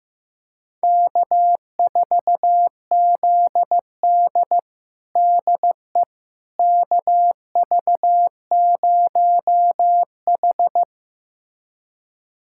Morse code, heard as a keyed tone, about 15 wpm, 700 Hz. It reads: K4ZD DE KV0H
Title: USA Callsign DE Callsign - 4 Characters - 15wpm